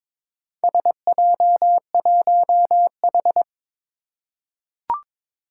Morse code, heard as a keyed tone, about 22 wpm, 700 Hz. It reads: SJ1H
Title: Four Challenging Letter-Number Combinations 22wpm